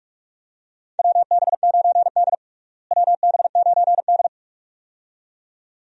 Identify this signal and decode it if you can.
Morse code, heard as a keyed tone, about 46 wpm, 700 Hz. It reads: WB9D WB9D
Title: USA Call Signs - 4 Characters - Repeated 2x - 45wpm